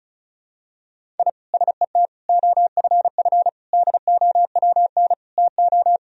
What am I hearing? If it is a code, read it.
Morse code, heard as a keyed tone, about 34 wpm, 700 Hz. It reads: I SET OFF DOWN TO
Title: Sentences from Top 1000 Words - Review Part 1 - 35wpm